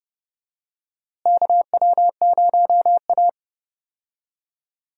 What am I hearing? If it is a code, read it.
Morse code, heard as a keyed tone, about 30 wpm, 700 Hz. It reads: KW0A
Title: USA Call Signs - 4 Characters - Repeated 3x - 30wpm